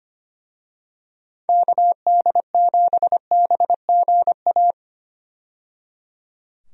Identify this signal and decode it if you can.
Morse code, heard as a keyed tone, about 25 wpm, 700 Hz. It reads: KD7BGA
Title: USA Call Signs - 6 Characters - 25wpm